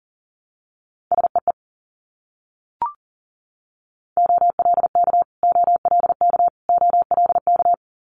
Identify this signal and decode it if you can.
Morse code, heard as a keyed tone, about 38 wpm, 700 Hz. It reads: SEE OLK OLK OLK
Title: Multi-letter Phonemes Repeated 3x 40wpm